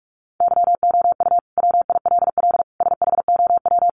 Morse code, heard as a keyed tone, about 44 wpm, 700 Hz. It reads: YOU WILL SHOW
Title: Sentences from Top 600 Words 45wpm